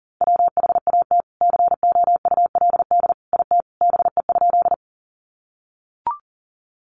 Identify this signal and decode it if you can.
Morse code, heard as a keyed tone, about 40 wpm, 700 Hz. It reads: WHAT COULD IT BE?
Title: Sentences from Top 100 Words 40wpm